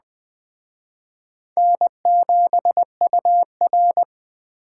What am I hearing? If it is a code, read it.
Morse code, heard as a keyed tone, about 20 wpm, 700 Hz. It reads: N7UR